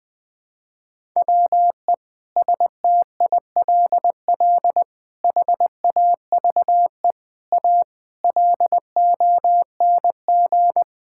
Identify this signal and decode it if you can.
Morse code, heard as a keyed tone, about 19 wpm, 700 Hz. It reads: WE STILL HAVE A LONG
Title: Sentences with 6-Word Phrases - Spoken Beginning - 20wpm